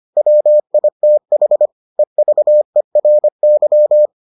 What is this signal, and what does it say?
Morse code, heard as a keyed tone, about 25 wpm, 600 Hz. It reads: WITH EVERY